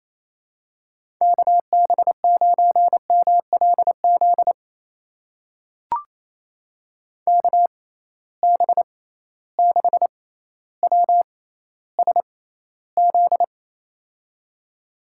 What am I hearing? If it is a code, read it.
Morse code, heard as a keyed tone, about 28 wpm, spaced as for 14 wpm, 700 Hz. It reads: KB9MLZ KB6WSZ
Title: USA Call Signs - 6 Characters - 6x Character Spacing - 28wpm